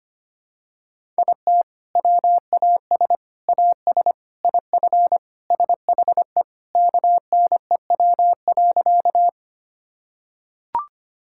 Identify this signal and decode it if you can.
Morse code, heard as a keyed tone, about 25 wpm, 700 Hz. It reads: IT WAS AS IF SHE KNEW.
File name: practice / 25wpm / Sentences with 4-Word Phrases - 25wpm